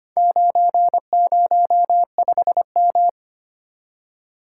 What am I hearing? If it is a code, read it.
Morse code, heard as a keyed tone, about 25 wpm, 700 Hz. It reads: 905M
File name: practice / 25wpm / Four Letter-Number Combinations - 25wpm